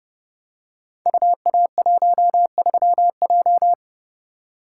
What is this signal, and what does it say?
Morse code, heard as a keyed tone, about 30 wpm, 700 Hz. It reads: UA13J
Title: Code Groups - 5 Letter:Numbers - 30wpm